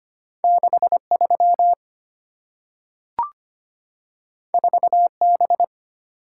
Morse code, heard as a keyed tone, about 25 wpm, 700 Hz. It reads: 63 4B
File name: Two Letter-Number Combinations - 25wpm